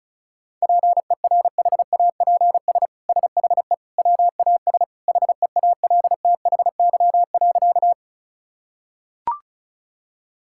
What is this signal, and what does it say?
Morse code, heard as a keyed tone, about 34 wpm, 700 Hz. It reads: PERHAPS SHE WAS HEALTHY.
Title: Sentences from Top 1000 Words - Review Part 5 - 35wpm